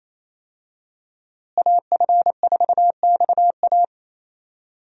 Morse code, heard as a keyed tone, about 28 wpm, 700 Hz. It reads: AF4XA
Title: USA Call Signs - 5 Characters - Repeated 2x - 28wpm